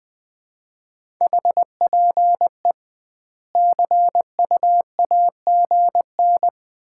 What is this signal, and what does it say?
Morse code, heard as a keyed tone, about 19 wpm, 700 Hz. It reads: HPE CU AGN
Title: Common QSO Phrases - 2x Word-Spacing - 20wpm